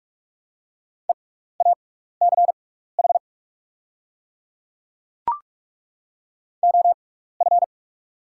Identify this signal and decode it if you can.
Morse code, heard as a keyed tone, about 48 wpm, spaced as for 16 wpm, 690 Hz. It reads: EACH OF
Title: 6x Character Spacing - Top 200 Words - 45wpm